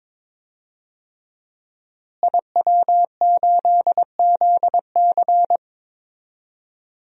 Morse code, heard as a keyed tone, about 22 wpm, 700 Hz. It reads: IW8ZC
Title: DXCC-100 Call Signs - 5 Characters - Repeated 3x - 22wpm